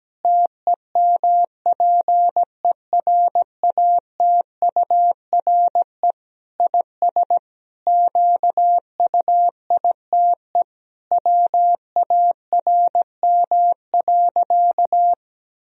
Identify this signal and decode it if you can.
Morse code, heard as a keyed tone, about 17 wpm, 700 Hz. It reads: TEMPERATURE IS QUITE WARM.
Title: Sentences from Top 1000 Words - Review Part 4 - 17wpm